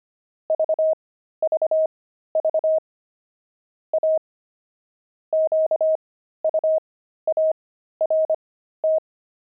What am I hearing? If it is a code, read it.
Morse code, heard as a keyed tone, about 25 wpm, spaced as for 13 wpm, 630 Hz. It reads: VVV A QUART
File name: pangram-fs-13-cs-25-oxide